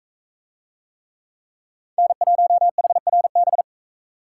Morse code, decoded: N1HRB

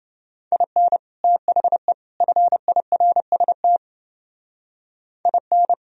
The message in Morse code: IN THE FIRST IN

